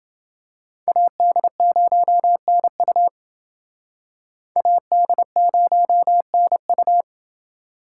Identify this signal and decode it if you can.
Morse code, heard as a keyed tone, about 28 wpm, 700 Hz. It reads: AD0NU AD0NU